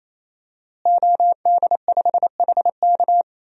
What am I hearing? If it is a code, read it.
Morse code, heard as a keyed tone, about 28 wpm, 700 Hz. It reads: OD5HK
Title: DXCC-100 Call Signs - 5 Characters - Repeated 3x - 28wpm